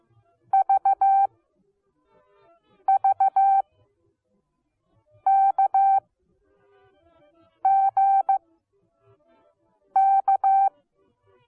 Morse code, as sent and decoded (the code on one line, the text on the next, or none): ...- ...- -.- --. -.-
VVKGK